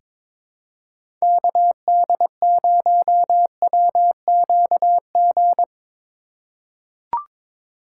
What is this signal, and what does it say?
Morse code, heard as a keyed tone, about 22 wpm, 700 Hz. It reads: KD0WQG
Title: USA Call Signs - 6x Character Spacing - 22wpm